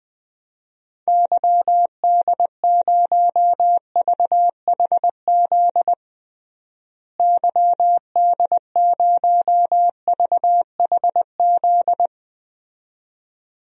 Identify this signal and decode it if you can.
Morse code, heard as a keyed tone, about 20 wpm, 700 Hz. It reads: YD0VHZ YD0VHZ
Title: DXCC-100 Call Signs - 6 Characters - Repeated 2x - 20wpm